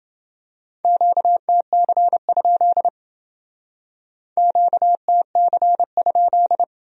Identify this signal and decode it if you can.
Morse code, heard as a keyed tone, about 28 wpm, 700 Hz. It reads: QTC? QTC?